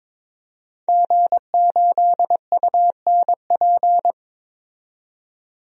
Morse code, heard as a keyed tone, about 21 wpm, 700 Hz. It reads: G8UNP
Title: DXCC-100 Call Signs - 5 Characters - 22wpm